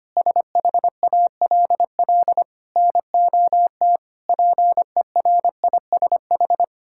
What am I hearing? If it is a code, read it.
Morse code, heard as a keyed tone, about 25 wpm, 700 Hz. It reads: SHALL NOT PERISH
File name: practice / 25wpm / Sentences with 9-Word Phrases - 25wpm